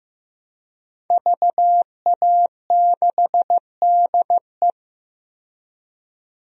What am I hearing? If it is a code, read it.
Morse code, heard as a keyed tone, about 15 wpm, 700 Hz. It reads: VA6DE
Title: DXCC-100 Call Signs - 5 Characters - 15wpm